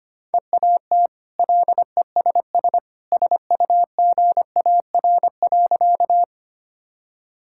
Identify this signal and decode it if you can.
Morse code, heard as a keyed tone, about 25 wpm, 690 Hz. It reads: EAT LESS SUGAR.
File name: Easy Sentences with Hints - 25wpm